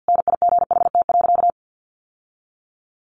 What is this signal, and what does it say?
Morse code, heard as a keyed tone, about 48 wpm, 700 Hz. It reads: NIGHT.
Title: Sentences from Top 1400 Words 50wpm